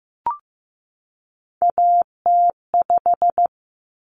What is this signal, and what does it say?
Morse code, heard as a keyed tone, about 15 wpm, 700 Hz. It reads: AT5
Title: Common Cut Numbers - 3-Digit 15wpm